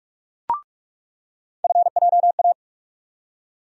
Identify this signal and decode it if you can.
Morse code, heard as a keyed tone, about 46 wpm, 700 Hz. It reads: FJA
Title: USA Call Sign Postfixes - 3 Letters 45wpm